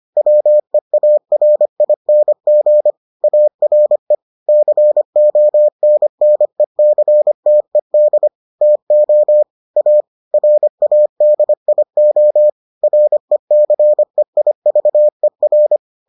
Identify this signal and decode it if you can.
Morse code, heard as a keyed tone, about 25 wpm, 600 Hz. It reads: WEARING ARE CONNECTED TO A RADIO RECEIVER